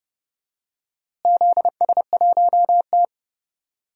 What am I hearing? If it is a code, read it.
Morse code, heard as a keyed tone, about 30 wpm, 700 Hz. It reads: ZS1T
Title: DXCC-100 Call Signs - 4 Characters - Repeated 3x - 30wpm